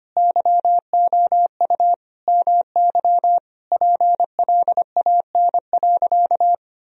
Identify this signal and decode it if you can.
Morse code, heard as a keyed tone, about 25 wpm, 700 Hz. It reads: YOU MY PLAN.
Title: Sentences from Top 1000 Words - Review Part 2 - 25wpm